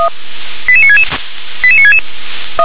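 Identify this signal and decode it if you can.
Five-tone selective call (ZVEI), heard as quick runs of five tones, tones of about 70 ms. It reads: 88966 88966
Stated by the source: ZVEI five-tone call received by SDR.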